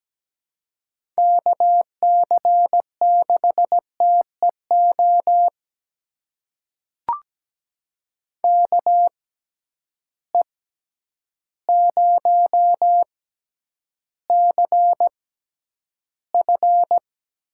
Morse code, heard as a keyed tone, about 17 wpm, spaced as for 9 wpm, 700 Hz. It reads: KC6TEO KE0CF
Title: USA Call Signs - 6 Characters - 6x Character Spacing - 17wpm